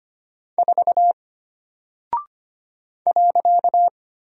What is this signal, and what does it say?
Morse code, heard as a keyed tone, about 25 wpm, 700 Hz. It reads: 4 .
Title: Single Character (Letter:Number:Punctuation:Prosign) Alt 25wpm